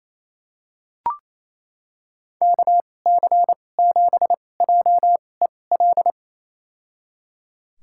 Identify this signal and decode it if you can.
Morse code, heard as a keyed tone, about 28 wpm, spaced as for 20 wpm, 700 Hz. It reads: KC7JEL
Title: USA Call Signs - 6 Characters - 2x Character Spacing - 28wpm